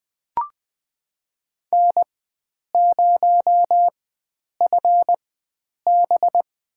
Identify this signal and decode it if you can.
Morse code, heard as a keyed tone, about 21 wpm, spaced as for 9 wpm, 700 Hz. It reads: N0FB
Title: USA Call Signs - 4x Character Spacing - 20wpm